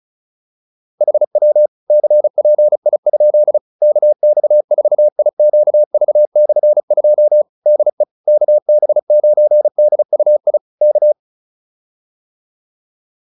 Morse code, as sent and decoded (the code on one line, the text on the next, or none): .... .-- / -.-. .--. .. ..--.. / -.- -..- ....- .. --.- ...- -..-. ..--- / -.. / -.- -... ----. -.. ..- .. / -.-
HW CPI? KX4IQV/2 DE KB9DUI K